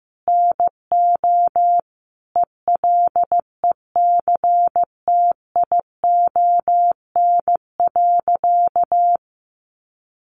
Morse code, heard as a keyed tone, about 15 wpm, 700 Hz. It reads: NO ELECTION.
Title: Sentences from Top 1500 Words 15wpm